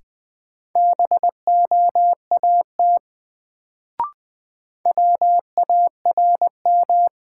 Morse code, heard as a keyed tone, about 20 wpm, 700 Hz. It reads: BOAT WARM
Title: Sentences from Top 500 Words - New Word Drill - 20wpm